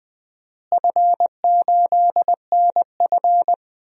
Morse code, heard as a keyed tone, about 20 wpm, 700 Hz. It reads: F8NF
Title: DXCC-100 Call Signs - 4 Characters - Repeated 3x - 20wpm